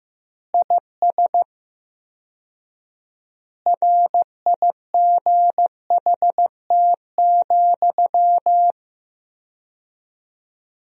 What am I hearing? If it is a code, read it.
Morse code, heard as a keyed tone, about 15 wpm, 700 Hz. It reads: IS RIGHT,
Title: 4x Word-Spacing - Sentences from Top 200 Words - 15wpm